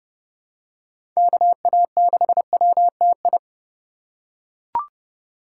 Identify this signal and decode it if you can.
Morse code, heard as a keyed tone, about 30 wpm, 700 Hz. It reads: KA6WTI
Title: USA Call Signs - 6 Characters - 4x Character Spacing - 30wpm